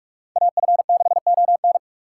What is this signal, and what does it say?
Morse code, heard as a keyed tone, about 45 wpm, 700 Hz. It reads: AF6ON
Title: USA Call Signs - 2x Character Spacing - 45wpm